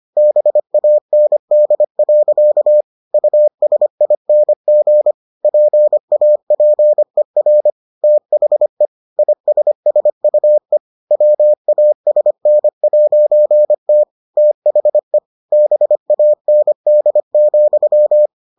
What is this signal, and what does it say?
Morse code, heard as a keyed tone, about 25 wpm, 600 Hz. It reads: BAND. USING PAPER THE ISSUE WASN'T THE BAND,